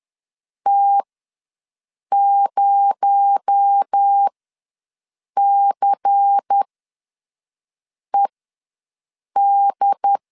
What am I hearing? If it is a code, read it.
Morse code, heard as a keyed tone, about 10 wpm, spaced as for 5 wpm, 800 Hz. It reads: T0CED
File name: LEZIONI / 6 PKX0 / 6 PKX0 ws 10wpm 4220